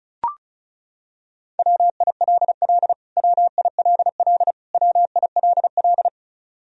Morse code, heard as a keyed tone, about 36 wpm, 700 Hz. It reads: WILL WILL WILL
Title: Top 100 Words Repeated 35wpm